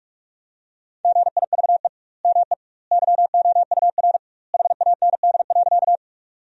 Morse code, heard as a keyed tone, about 46 wpm, 700 Hz. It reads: GIVE ME YOUR HAND.
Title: 2x Word-Spacing - Sentences from Top 200 Words - 45wpm